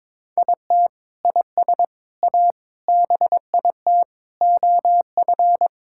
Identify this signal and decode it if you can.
Morse code, heard as a keyed tone, about 22 wpm, 700 Hz. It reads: IT IS A BIT OF
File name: practice / 22wpm / Sentences with 4-Word Phrases - 22wpm